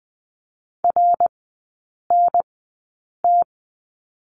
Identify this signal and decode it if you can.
Morse code, heard as a keyed tone, about 20 wpm, spaced as for 8 wpm, 700 Hz. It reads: RNT
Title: 053 Lesson 10e - Letter r - Three Characters 20wpm